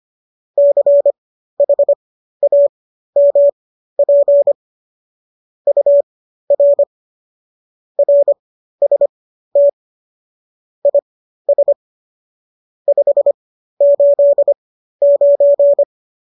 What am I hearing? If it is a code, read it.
Morse code, heard as a keyed tone, about 25 wpm, spaced as for 13 wpm, 570 Hz. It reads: CHAMP UR RST IS 589